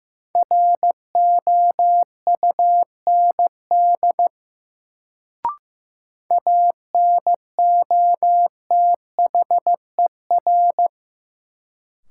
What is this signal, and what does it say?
Morse code, heard as a keyed tone, about 15 wpm, 700 Hz. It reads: ROUND ANOTHER
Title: Sets of 5 Words Encore - New Word Drill - 15wpm